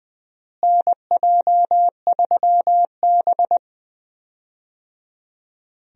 Morse code, heard as a keyed tone, about 20 wpm, 700 Hz. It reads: NJ3B
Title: USA Call Signs - 4 Characters - Repeated 3x - 20wpm